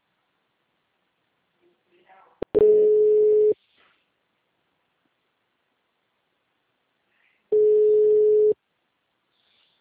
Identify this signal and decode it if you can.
Ringback tone (Europe, 425 Hz), with speech.